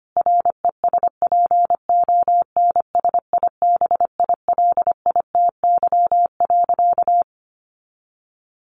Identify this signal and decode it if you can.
Morse code, heard as a keyed tone, about 25 wpm, 700 Hz. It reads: RESPONSIBILITY.
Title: Sentences from Top 1500 Words 25wpm